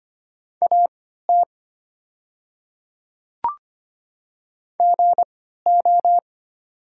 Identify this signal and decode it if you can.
Morse code, heard as a keyed tone, about 25 wpm, spaced as for 14 wpm, 700 Hz. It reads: AT GO
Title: 3x Character Spacing - Top 100 Words - 25wpm